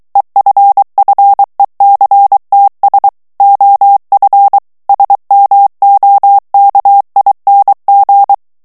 Morse code, heard as a keyed tone, about 23 wpm, 800 Hz. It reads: EFFECTS OF SMOKING
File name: morse mp3 / ANGLAIS texte clair 1 vitesse 22 mots minute